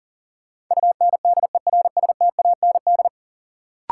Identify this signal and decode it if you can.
Morse code, heard as a keyed tone, about 38 wpm, 700 Hz. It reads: UNDERSTAND
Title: Sets of 4 Words - New Word Drill - 40wpm